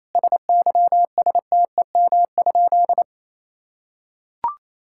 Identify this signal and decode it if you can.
Morse code, heard as a keyed tone, about 28 wpm, 700 Hz. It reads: SYSTEM?
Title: Sentences from Top 1000 Words - Review Part 5 - 28wpm